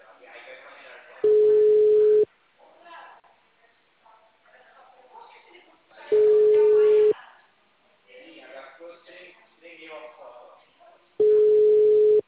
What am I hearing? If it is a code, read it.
Ringback tone (Europe, 425 Hz), with speech.